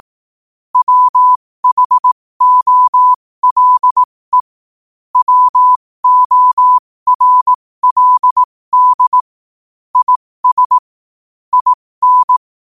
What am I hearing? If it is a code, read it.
Morse code, heard as a keyed tone, about 18 wpm, spaced as for 15 wpm, 1000 Hz. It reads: WHOLE WORLD IS IN
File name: qotdfiles / 15wpm